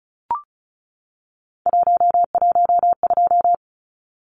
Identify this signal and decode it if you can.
Morse code, heard as a keyed tone, about 34 wpm, 700 Hz. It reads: 112
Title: Numbers - 3-Digit 35wpm